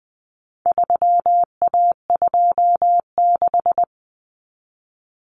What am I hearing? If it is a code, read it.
Morse code, heard as a keyed tone, about 20 wpm, 700 Hz. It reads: 3A26
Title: Common Cut Numbers - 4-Digit 20wpm